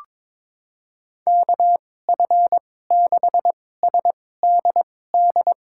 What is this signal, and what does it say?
Morse code, heard as a keyed tone, about 22 wpm, spaced as for 16 wpm, 700 Hz. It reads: KF6SDD
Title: USA Call Signs - 6 Characters - 2x Character Spacing - 22wpm